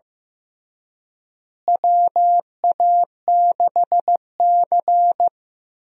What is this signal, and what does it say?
Morse code, heard as a keyed tone, about 15 wpm, 700 Hz. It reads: WA6C
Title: USA Call Signs - 4 Characters - Repeated 3x - 15wpm